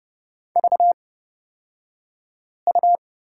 Morse code, heard as a keyed tone, about 30 wpm, spaced as for 5 wpm, 700 Hz. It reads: VU
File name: Lesson 38d - Letter q - Two Characters 30wpm